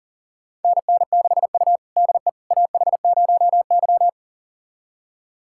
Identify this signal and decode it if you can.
Morse code, heard as a keyed tone, about 37 wpm, 700 Hz. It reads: NN6U DE AH0Y
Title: USA Callsign DE Callsign - 4 Characters - 40wpm